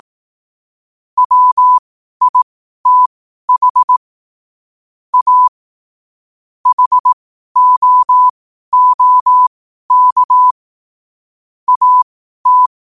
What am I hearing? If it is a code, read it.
Morse code, heard as a keyed tone, about 18 wpm, spaced as for 12 wpm, 1000 Hz. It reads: WITH A HOOK AT
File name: qotdfiles / 12wpm